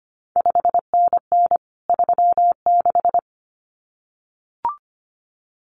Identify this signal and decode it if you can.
Morse code, heard as a keyed tone, about 25 wpm, 700 Hz. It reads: 5NN 36
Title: CQ WW DX Contest Exchange Encore - 25wpm